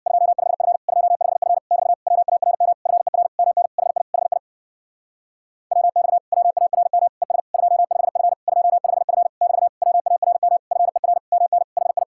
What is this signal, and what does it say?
Morse code, heard as a keyed tone, about 67 wpm, 700 Hz. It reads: 25F 25F <BT> WARM FER MN HI HI WX WARM ES 25F 25F <BT> WARM FER MN HI